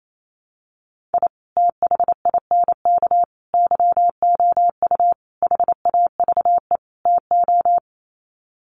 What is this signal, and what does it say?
Morse code, heard as a keyed tone, about 28 wpm, 700 Hz. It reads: I THINK YOU HAVE TO
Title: Sets of 5 Words 28wpm